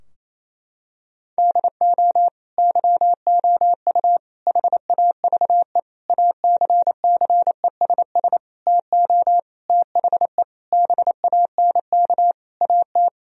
Morse code, heard as a keyed tone, about 28 wpm, 700 Hz. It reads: DO YOU HAVE ACCESS TO THE BANK AT
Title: Sentences from Top 1000 Words - Review Part 4 - 28wpm